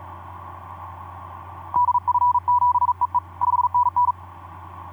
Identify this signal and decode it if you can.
Morse code, heard as a keyed tone, about 37 wpm, 980 Hz. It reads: DF7EE 5NN
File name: d44ac ek2rgt etc 10m cw 23nov2010 1202